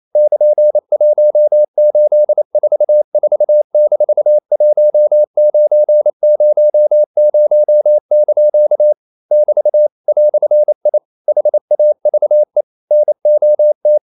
Morse code, heard as a keyed tone, about 28 wpm, 600 Hz. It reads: <KN>1844-1900) <BT> "I HAVE NOT
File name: CW / Practice / EnglishQuotations0228